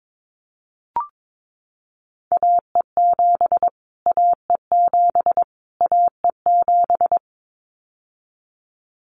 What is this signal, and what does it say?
Morse code, heard as a keyed tone, about 22 wpm, 700 Hz. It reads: AE7 AE7 AE7
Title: Callsign Prefixes for Top 25 DXCC - Repeated 3x - 22wpm